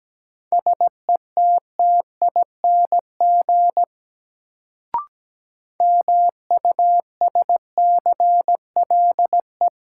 Morse code, heard as a keyed tone, about 17 wpm, 700 Hz. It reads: SETTING MUSCLE
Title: Sentences from Top 1300 Words - New Word Drill - 17wpm